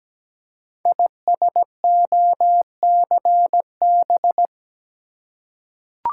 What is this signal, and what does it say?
Morse code, heard as a keyed tone, about 17 wpm, 700 Hz. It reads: ISOCB
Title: Code Groups - 5 Letters - 17wpm